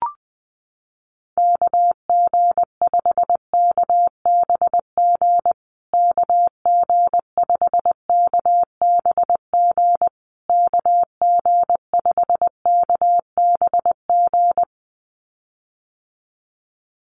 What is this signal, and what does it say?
Morse code, heard as a keyed tone, about 20 wpm, 700 Hz. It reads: KG5KBG KG5KBG KG5KBG
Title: USA Call Signs Repeated 3x 20wpm